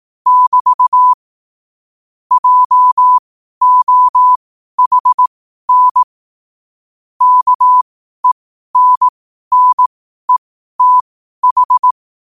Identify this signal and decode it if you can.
Morse code, heard as a keyed tone, about 18 wpm, spaced as for 12 wpm, 1000 Hz. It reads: <BT> JOHN KENNETH
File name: qotdfiles / 12wpm